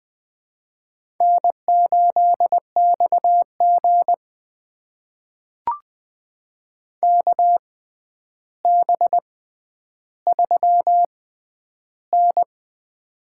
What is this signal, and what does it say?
Morse code, heard as a keyed tone, about 20 wpm, spaced as for 10 wpm, 700 Hz. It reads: N8XG KB3N
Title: USA Call Signs - 4 Characters - 6x Character Spacing - 20wpm